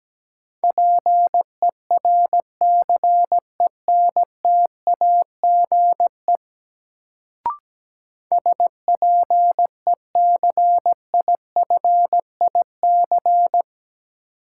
Morse code, heard as a keyed tone, about 17 wpm, 700 Hz. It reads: PERCENTAGE SPECIFIC
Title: Sentences from Top 1300 Words - New Word Drill Alt - 17wpm